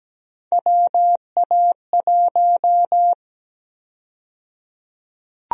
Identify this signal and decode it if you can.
Morse code, heard as a keyed tone, about 17 wpm, 700 Hz. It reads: WA1
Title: USA Call Sign Prefixes - 17wpm